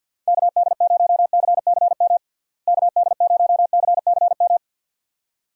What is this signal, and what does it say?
Morse code, heard as a keyed tone, about 50 wpm, 700 Hz. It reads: KD0CCM KD0CCM
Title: USA Call Signs - 6 Characters - Repeated 2x - 50wpm